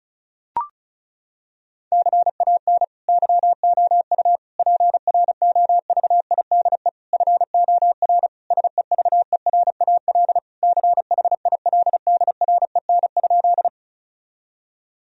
Morse code, heard as a keyed tone, about 36 wpm, 700 Hz. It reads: CAN YOU PROVIDE FOR SEVERAL CHILDREN?
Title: Sentences from Top 1000 Words - Review Part 2 - 35wpm